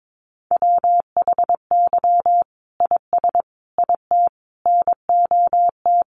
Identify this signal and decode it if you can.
Morse code, heard as a keyed tone, about 22 wpm, 700 Hz. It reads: WHY IS IT NOT
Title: Sentences from Top 1400 Words 22wpm